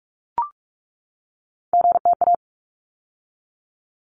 Morse code, heard as a keyed tone, about 44 wpm, 700 Hz. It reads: GTA
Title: ARRL Sections 45wpm